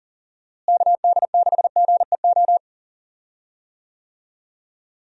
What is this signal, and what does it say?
Morse code, heard as a keyed tone, about 39 wpm, 700 Hz. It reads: KD6GEO
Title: USA Call Signs - 6 Characters - Repeated 2x - 40wpm